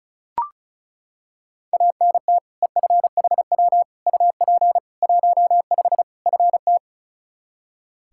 Morse code, heard as a keyed tone, about 34 wpm, 700 Hz. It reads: ANT EFHW UP 15 FT
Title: Practice QSO - Exchange 2 - Info - 35wpm